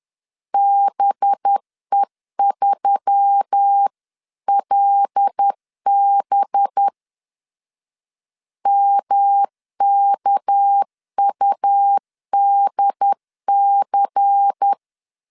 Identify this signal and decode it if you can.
Morse code, heard as a keyed tone, about 10 wpm, 800 Hz. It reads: BE3LB MKUDC
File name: LEZIONI / 6 PKX0 / 6 PK 10wpm 2436